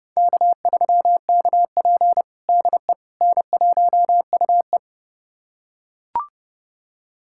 Morse code, heard as a keyed tone, about 31 wpm, 700 Hz. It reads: K3KP DE N1UE
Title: USA Callsign DE Callsign - 4 Characters - 30wpm